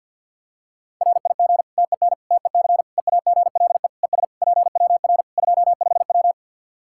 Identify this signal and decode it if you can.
Morse code, heard as a keyed tone, about 49 wpm, 700 Hz. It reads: RIG TEN TEC EAGLE ES PWR 25W
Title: Practice QSO - Exchange 2 - Info - 50wpm